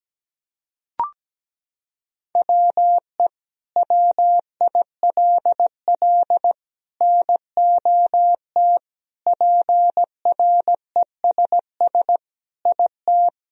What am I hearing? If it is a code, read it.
Morse code, heard as a keyed tone, about 17 wpm, 700 Hz. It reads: WE WILL NOT PRESS IT
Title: Sentences from Top 1000 Words - Review Part 1 - 17wpm